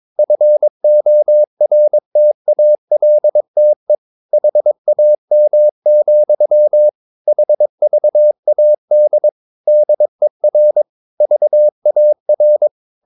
Morse code, heard as a keyed tone, about 22 wpm, 600 Hz. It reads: FORTALTE HAM, HVAD DER VAR
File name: Ferie QRS 22wpm